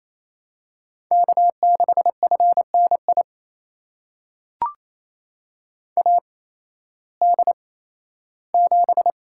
Morse code, heard as a keyed tone, about 28 wpm, spaced as for 15 wpm, 700 Hz. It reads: K6FNI AD7
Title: USA Call Signs - 8x Character Spacing - 28wpm